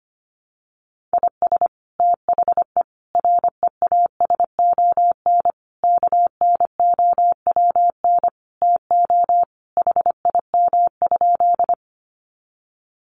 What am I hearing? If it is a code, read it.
Morse code, heard as a keyed tone, about 25 wpm, 700 Hz. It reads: IS THE REASON KNOWN TO HIM?
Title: Sentences from Top 700 Words 25wpm